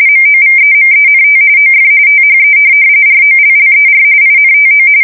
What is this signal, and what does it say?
RTTY, about 45.45 baud. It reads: MOVEMENT OF THE ROLLER INTO
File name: fsk170 / morkrum